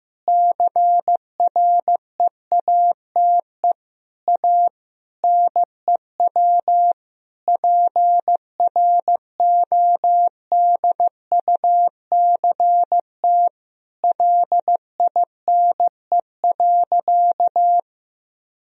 Morse code, heard as a keyed tone, about 15 wpm, 700 Hz. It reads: CREATE A NEW PRODUCT LINE.
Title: Sentences from Top 1000 Words - Review Part 3 - 15wpm